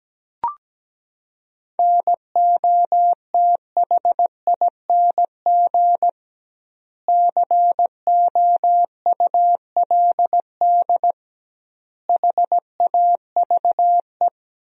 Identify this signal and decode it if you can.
Morse code, heard as a keyed tone, about 17 wpm, 700 Hz. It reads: NOTHING COULD HAVE
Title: 2x Word-Spacing - Sentences from Top 500 Words - 17wpm